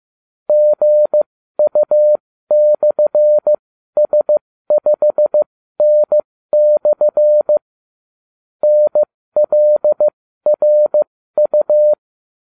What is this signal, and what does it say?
Morse code, heard as a keyed tone, about 15 wpm, spaced as for 13 wpm, 600 Hz. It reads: GU/S5N/ NLRU